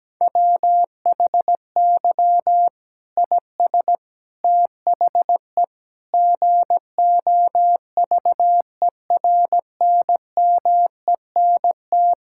Morse code, heard as a keyed tone, about 17 wpm, 700 Hz. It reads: WHY IS THE GOVERNMENT
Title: Sentences from Top 1000 Words - Review Part 5 - 17wpm